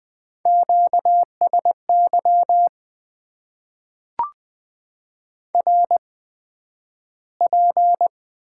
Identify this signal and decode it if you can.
Morse code, heard as a keyed tone, about 20 wpm, spaced as for 11 wpm, 700 Hz. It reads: QSY RP
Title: Top 90 QSO Elements - 8x Character Spacing - 20wpm